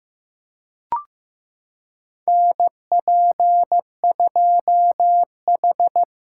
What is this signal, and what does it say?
Morse code, heard as a keyed tone, about 15 wpm, 700 Hz. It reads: NP2H
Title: USA Callsign DE Callsign - 4 Characters - 15wpm